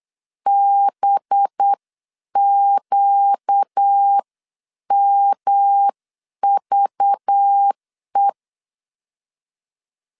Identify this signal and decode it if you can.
Morse code, heard as a keyed tone, about 8 wpm, 800 Hz. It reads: BQMVE
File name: 8wpm